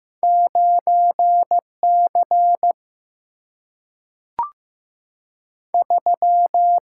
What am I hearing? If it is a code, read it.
Morse code, heard as a keyed tone, about 15 wpm, 700 Hz. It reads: 9C 3
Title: Single-Letter Single-Number 15wpm